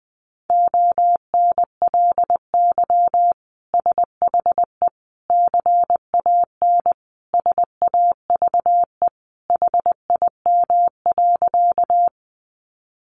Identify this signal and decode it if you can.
Morse code, heard as a keyed tone, about 20 wpm, 700 Hz. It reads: ONLY SHE CAN SAVE HIM.
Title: Sentences from Top 1100 Words 20wpm